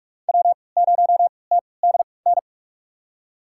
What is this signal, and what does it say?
Morse code, heard as a keyed tone, about 43 wpm, spaced as for 25 wpm, 700 Hz. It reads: W0TDN
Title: USA Call Signs - 5 Characters - 3x Character Spacing - 45wpm